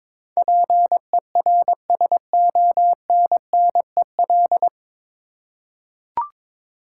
Morse code, heard as a keyed tone, about 22 wpm, 700 Hz. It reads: PERSONNEL
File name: practice / 22wpm / 9 Letter Words - First 3 Letters Given - 22wpm